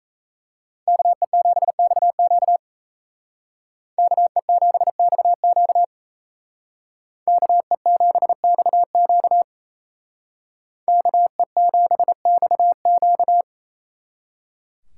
Morse code, heard as a keyed tone, about 36 wpm, 700 Hz. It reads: KE7XQ KE7XQ KE7XQ KE7XQ